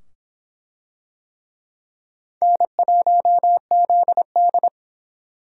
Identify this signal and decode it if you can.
Morse code, heard as a keyed tone, about 24 wpm, 700 Hz. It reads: N1ZD